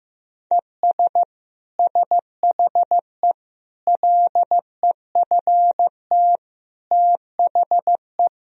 Morse code, heard as a keyed tone, about 16 wpm, 700 Hz. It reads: ES SHE LEFT THE
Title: Sentences from Top 300 Words and Abbreviations 15wpm